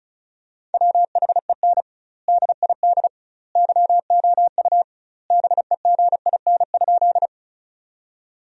Morse code, heard as a keyed tone, about 34 wpm, 700 Hz. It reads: WHEN DID YOU BEGIN?
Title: 2x Word-Spacing - Sentences from Top 300 Words - 35wpm